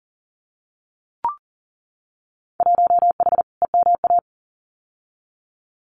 Morse code, heard as a keyed tone, about 38 wpm, 700 Hz. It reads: 1H EMA
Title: Winter Field Day Exchanges 40wpm